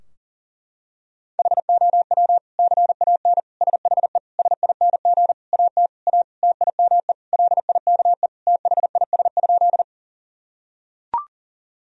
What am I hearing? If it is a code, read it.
Morse code, heard as a keyed tone, about 40 wpm, 700 Hz. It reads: HOW CAN SHE SING AT A TIME LIKE THIS?
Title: Sentences from Top 1000 Words - Review Part 2 - 40wpm